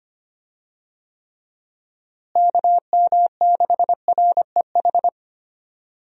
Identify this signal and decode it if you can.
Morse code, heard as a keyed tone, about 25 wpm, 700 Hz. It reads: KM6REH
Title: USA Call Signs - 6 Characters - Repeated 2x - 25wpm